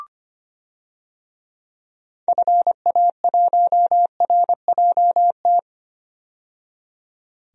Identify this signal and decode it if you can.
Morse code, heard as a keyed tone, about 25 wpm, 700 Hz. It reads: FA1RJT
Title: DXCC-100 Call Signs - 6 Characters - 25wpm